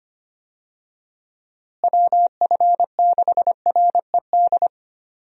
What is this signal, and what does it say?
Morse code, heard as a keyed tone, about 25 wpm, 700 Hz. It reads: WF6RED
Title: USA Call Signs - 6 Characters - Repeated 3x - 25wpm